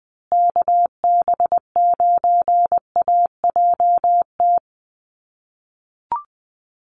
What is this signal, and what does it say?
Morse code, heard as a keyed tone, about 20 wpm, 700 Hz. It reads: KB9AJT
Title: USA Call Signs Repeated 3x 20wpm